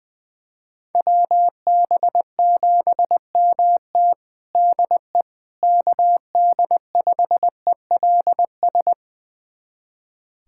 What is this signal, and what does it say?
Morse code, heard as a keyed tone, about 20 wpm, 700 Hz. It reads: WB7MT DE KD5ELS
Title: USA Callsign DE Callsign 20wpm